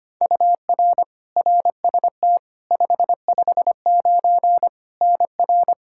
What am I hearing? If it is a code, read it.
Morse code, heard as a keyed tone, about 25 wpm, 700 Hz. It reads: UR RST 559 NR
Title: K1USN SST QSO Practice 25wpm